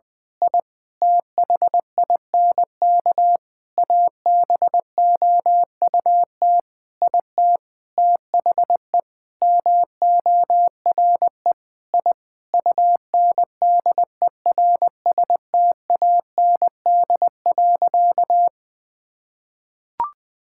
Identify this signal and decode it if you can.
Morse code, heard as a keyed tone, about 21 wpm, 700 Hz. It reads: I THINK ABOUT IT THE MORE I UNDERSTAND.
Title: Sentences with 8-Word Phrases - 20wpm